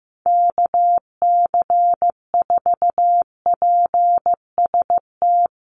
Morse code, heard as a keyed tone, about 15 wpm, 700 Hz. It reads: KC4PST
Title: USA Call Signs Repeated 3x 15wpm